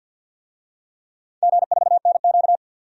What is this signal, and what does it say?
Morse code, heard as a keyed tone, about 51 wpm, 700 Hz. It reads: G4NQ
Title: DXCC-100 Call Signs - 4 Characters - 50wpm